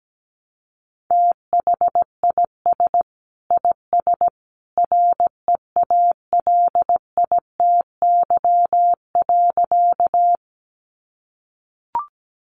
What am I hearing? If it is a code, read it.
Morse code, heard as a keyed tone, about 17 wpm, 700 Hz. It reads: THIS IS REALITY.
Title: Sentences from Top 1200 Words 17wpm